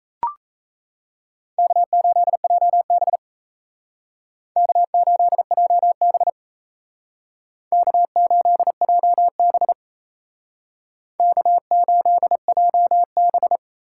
Morse code, heard as a keyed tone, about 34 wpm, 700 Hz. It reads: K8JB K8JB K8JB K8JB
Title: Speed Racing - USA Call Signs 28wpm